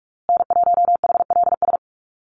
Morse code, heard as a keyed tone, about 46 wpm, 700 Hz. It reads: N1HLS
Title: USA Call Signs Repeated 3x 45wpm